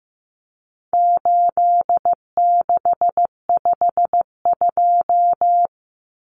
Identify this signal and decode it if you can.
Morse code, heard as a keyed tone, about 15 wpm, 700 Hz. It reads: 8652
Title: Common Cut Numbers - 4-Digit 15wpm